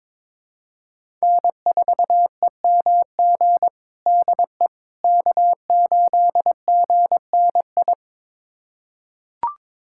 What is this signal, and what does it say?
Morse code, heard as a keyed tone, about 21 wpm, 700 Hz. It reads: N4EMG DE K8GNI
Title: USA Callsign DE Callsign - 5 Characters - 22wpm